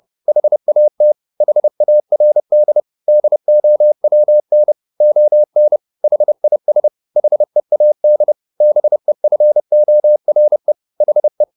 Morse code, heard as a keyed tone, about 30 wpm, 600 Hz. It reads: HAT HARD DOWN ON HIS HEAD BEFORE HE
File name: the old man and the sea / the old man and the sea 1 30wpm